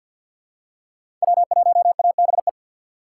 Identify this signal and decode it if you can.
Morse code, heard as a keyed tone, about 50 wpm, 700 Hz. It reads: W1ABE
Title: USA Call Signs - 5 Characters - 2x Character Spacing - 50wpm